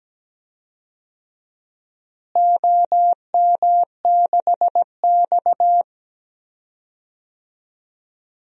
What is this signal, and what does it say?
Morse code, heard as a keyed tone, about 17 wpm, 700 Hz. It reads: OM6X
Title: DXCC-100 Call Signs - 4 Characters - Repeated 3x - 17wpm